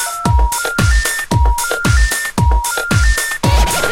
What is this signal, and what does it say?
Telephone call progress tone, heard as a telephone network's tone, with noise. Special information tone.